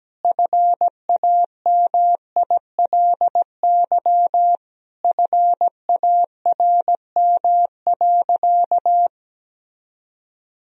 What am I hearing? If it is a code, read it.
Morse code, heard as a keyed tone, about 17 wpm, 700 Hz. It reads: FAMILY FARM.
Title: Sentences from Top 1000 Words - Review Part 2 - 17wpm